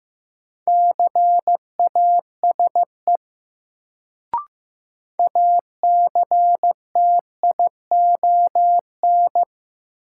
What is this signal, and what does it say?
Morse code, heard as a keyed tone, about 15 wpm, 700 Hz. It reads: CASE ACTION
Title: Sentences from Top 700 Words - New Word Drill - 15wpm